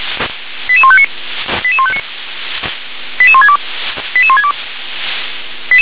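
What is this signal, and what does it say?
Five-tone selective call (ZVEI), heard as quick runs of five tones, tones of about 70 ms. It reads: 88159 88159 88162 88162
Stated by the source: ZVEI five-tone selective call received on SDR.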